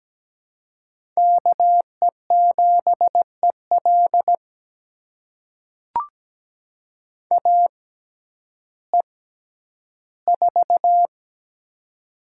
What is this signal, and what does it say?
Morse code, heard as a keyed tone, about 17 wpm, spaced as for 10 wpm, 700 Hz. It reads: KE7EL AE4
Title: USA Call Signs - 5 Characters - 6x Character Spacing - 17wpm